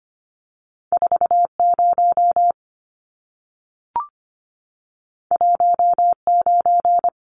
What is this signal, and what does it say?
Morse code, heard as a keyed tone, about 25 wpm, 700 Hz. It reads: 40 19
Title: Numbers - 2-Digit 25wpm